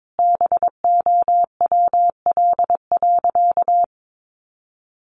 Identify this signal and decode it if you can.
Morse code, heard as a keyed tone, about 22 wpm, 700 Hz. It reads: BOWL.
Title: Sentences from Top 1200 Words 22wpm